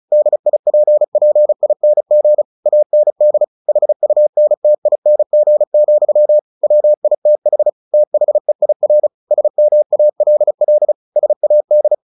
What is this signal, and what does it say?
Morse code, heard as a keyed tone, about 34 wpm, 600 Hz. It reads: DIPPING AND HUNTING, WITH THEIR SMALL SAD
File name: the old man and the sea / the old man and the sea 1 35wpm